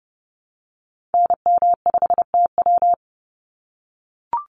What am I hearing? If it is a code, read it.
Morse code, heard as a keyed tone, about 28 wpm, 700 Hz. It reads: NM5TW
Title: NA SOTA Activator Callsigns 30wpm